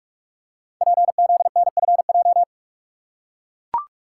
Morse code, heard as a keyed tone, about 44 wpm, 700 Hz. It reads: PZNFJ
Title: Code Groups - 5 Letters - 45wpm